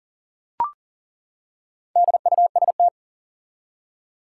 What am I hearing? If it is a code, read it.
Morse code, heard as a keyed tone, about 40 wpm, 700 Hz. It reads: DUST
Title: Four Letter Words - 40wpm